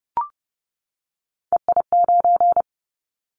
Morse code, heard as a keyed tone, about 30 wpm, 700 Hz. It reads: EI9
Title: Callsign Prefixes for Top 50 DXCC - 30wpm